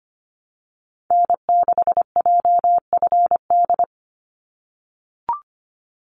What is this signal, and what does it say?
Morse code, heard as a keyed tone, about 25 wpm, 700 Hz. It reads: N6JFD
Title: NA SOTA Activator Callsigns 25wpm